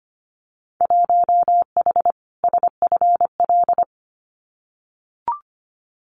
Morse code, heard as a keyed tone, about 25 wpm, 700 Hz. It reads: 1H SFL
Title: Winter Field Day Exchanges 25wpm